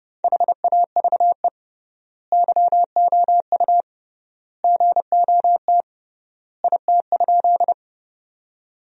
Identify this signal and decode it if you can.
Morse code, heard as a keyed tone, about 30 wpm, 700 Hz. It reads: HAVE YOU GOT IT?
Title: 3x Word-Spacing - Sentences from Top 300 Words - 30wpm